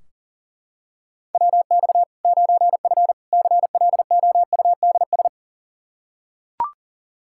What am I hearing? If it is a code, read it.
Morse code, heard as a keyed tone, about 40 wpm, 700 Hz. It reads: WX 9F CLOUDS
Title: Practice QSO - Exchange 2 - Info - 40wpm